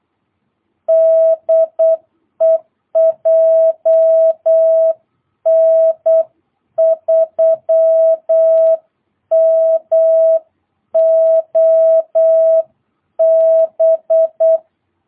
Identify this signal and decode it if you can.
Morse code, heard as a keyed tone, about 8 wpm, 650 Hz. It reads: DE JN3MOB
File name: jh8cbh / 2022cw